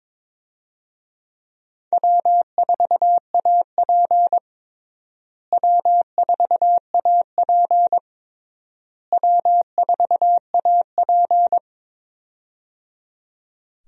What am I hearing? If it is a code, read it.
Morse code, heard as a keyed tone, about 22 wpm, 700 Hz. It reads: W4AP W4AP W4AP
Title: USA Call Signs - 4 Characters - Repeated 3x - 22wpm